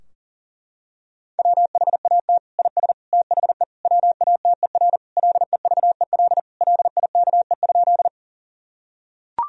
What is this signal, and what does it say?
Morse code, heard as a keyed tone, about 40 wpm, 700 Hz. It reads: WHAT IS THE WATER LEVEL LIKE?
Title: Sentences from Top 1000 Words - Review Part 4 - 40wpm